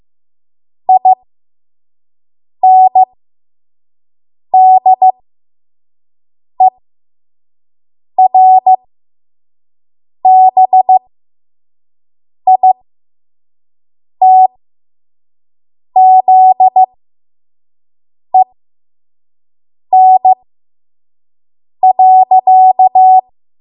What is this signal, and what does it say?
Morse code, heard as a keyed tone, about 15 wpm, spaced as for 5 wpm, 750 Hz. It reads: INDERBITZEN.
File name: Morse / 05WPM